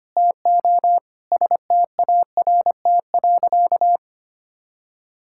Morse code, heard as a keyed tone, about 25 wpm, 700 Hz. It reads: TO START.
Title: Sentences from Top 1000 Words - Review Part 5 - 25wpm